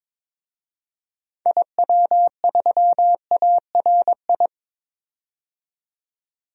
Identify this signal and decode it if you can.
Morse code, heard as a keyed tone, about 22 wpm, 700 Hz. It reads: IW3ARI
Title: DXCC-100 Call Signs - 6 Characters - 22wpm